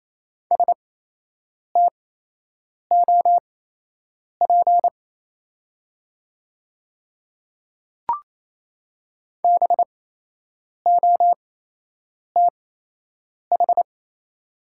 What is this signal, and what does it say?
Morse code, heard as a keyed tone, about 28 wpm, spaced as for 8 wpm, 700 Hz. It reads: STOP BOTH
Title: 8x Character Spacing - Top 300 Words - 28wpm